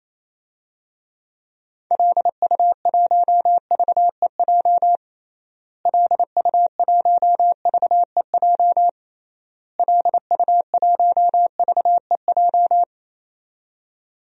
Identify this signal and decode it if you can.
Morse code, heard as a keyed tone, about 28 wpm, 700 Hz. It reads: LU1VEJ LU1VEJ LU1VEJ
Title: DXCC-100 Call Signs - 6 Characters - Repeated 3x - 28wpm